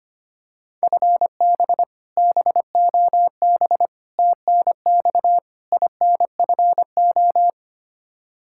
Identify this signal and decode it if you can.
Morse code, heard as a keyed tone, about 25 wpm, 700 Hz. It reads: FB BOB TNX INFO
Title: K1USN SST QSO Practice 25wpm